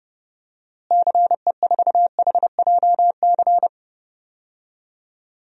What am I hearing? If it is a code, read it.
Morse code, heard as a keyed tone, about 30 wpm, 700 Hz. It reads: CE4HJC
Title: DXCC-100 Call Signs - 6 Characters - Repeated 2x - 30wpm